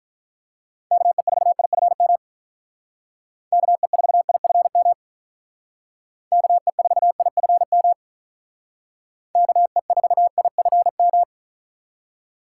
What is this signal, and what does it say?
Morse code, heard as a keyed tone, about 43 wpm, 700 Hz. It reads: KE4IFM KE4IFM KE4IFM KE4IFM
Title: Speed Racing - USA 2x3 Call Signs 35wpm